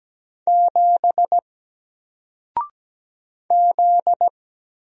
Morse code, heard as a keyed tone, about 17 wpm, 700 Hz. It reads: 7 Z
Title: Single Character (Letter:Number:Punctuation:Prosign) Alt 17wpm